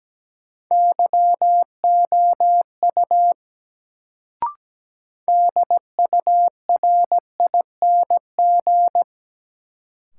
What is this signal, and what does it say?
Morse code, heard as a keyed tone, about 17 wpm, 700 Hz. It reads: YOU DURING
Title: Sets of 2 Words - New Word Drill - 17wpm